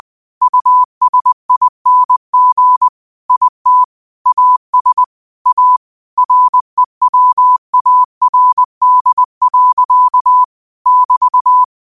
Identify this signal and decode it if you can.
Morse code, heard as a keyed tone, about 20 wpm, 1000 Hz. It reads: USING IT AS A REWARD. <BT>